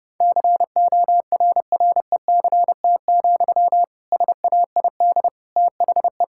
Morse code, heard as a keyed tone, about 30 wpm, 700 Hz. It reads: CORRECT, SAID THE
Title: Sentences from Top 1600 Words 30wpm